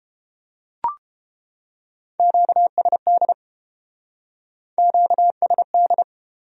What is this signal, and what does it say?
Morse code, heard as a keyed tone, about 29 wpm, 700 Hz. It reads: QSD QSD